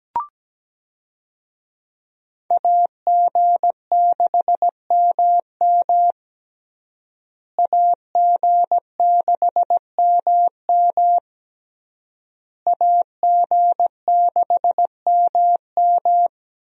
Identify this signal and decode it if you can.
Morse code, heard as a keyed tone, about 17 wpm, 700 Hz. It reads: AG6MM AG6MM AG6MM
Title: USA Call Signs - 5 Characters - Repeated 3x - 17wpm